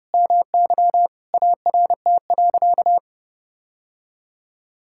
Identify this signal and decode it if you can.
Morse code, heard as a keyed tone, about 30 wpm, 700 Hz. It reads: MY ART.
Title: Sentences from Top 1000 Words - Review Part 6 - 30wpm